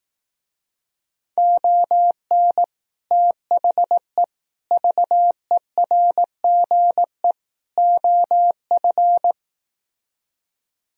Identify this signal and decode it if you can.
Morse code, heard as a keyed tone, about 18 wpm, 700 Hz. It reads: ON THE VERGE OF
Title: Speed-Racing - Sets of 4 Words 15wpm